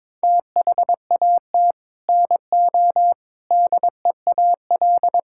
Morse code, heard as a keyed tone, about 22 wpm, 700 Hz. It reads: THAT NO DEAL
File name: Sets of 9 Words - 22wpm